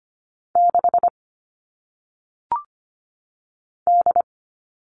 Morse code, heard as a keyed tone, about 25 wpm, 700 Hz. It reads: 6 D
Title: All Cut Numbers - 1-Digit 25wpm